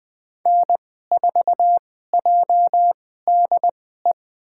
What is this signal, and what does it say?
Morse code, heard as a keyed tone, about 20 wpm, spaced as for 14 wpm, 700 Hz. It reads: N4JDE
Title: USA Call Signs - 2x Character Spacing - 20wpm